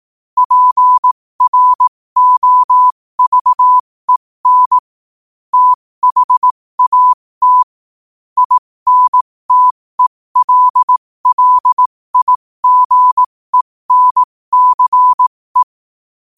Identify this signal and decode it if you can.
Morse code, heard as a keyed tone, about 18 wpm, spaced as for 15 wpm, 1000 Hz. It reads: PROVEN THAT INTELLIGENCE
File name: qotdfiles / 15wpm